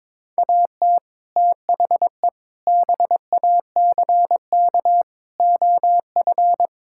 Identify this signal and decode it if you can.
Morse code, heard as a keyed tone, about 21 wpm, 700 Hz. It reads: AT THE BACK OF
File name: practice / 22wpm / Build A Phrase - 5-Word Phrases - 22wpm